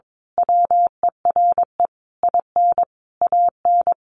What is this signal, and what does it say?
Morse code, heard as a keyed tone, about 22 wpm, 700 Hz. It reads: WERE IN AN
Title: A Sound Of Thunder 22wpm